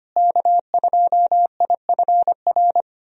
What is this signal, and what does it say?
Morse code, heard as a keyed tone, about 25 wpm, 700 Hz. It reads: K2IFR
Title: USA Call Signs - 5 Characters - Repeated 3x - 25wpm